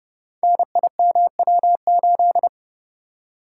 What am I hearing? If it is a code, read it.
Morse code, heard as a keyed tone, about 30 wpm, 700 Hz. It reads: NIMW8
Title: Code Groups - 5 Letter:Numbers - 30wpm